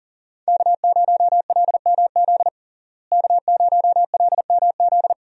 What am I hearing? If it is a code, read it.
Morse code, heard as a keyed tone, about 40 wpm, 700 Hz. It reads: K0LMZ K0LMZ
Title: USA Call Signs - 5 Characters - Repeated 3x - 40wpm